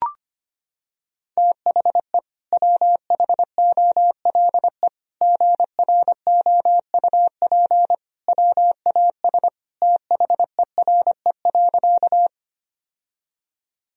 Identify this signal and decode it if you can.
Morse code, heard as a keyed tone, about 25 wpm, 700 Hz. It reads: THE WHOLE GROUP WAS THERE.
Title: Sentences from Top 1000 Words - Review Part 1 - 25wpm